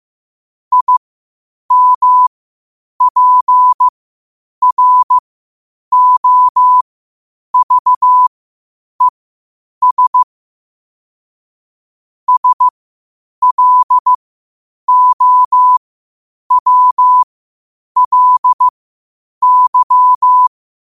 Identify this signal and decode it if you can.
Morse code, heard as a keyed tone, about 15 wpm, spaced as for 8 wpm, 1000 Hz. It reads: IMPROVES SLOWLY
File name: qotdfiles / 8wpm